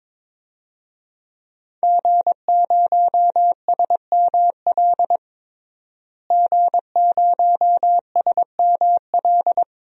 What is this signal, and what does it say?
Morse code, heard as a keyed tone, about 22 wpm, 700 Hz. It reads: G0SML G0SML
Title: DXCC-100 Call Signs - 5 Characters - Repeated 2x - 22wpm